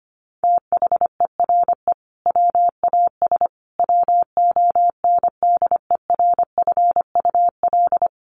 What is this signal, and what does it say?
Morse code, heard as a keyed tone, about 25 wpm, 700 Hz. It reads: THERE WAS WONDERFUL
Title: Sentences from Top 1200 Words 25wpm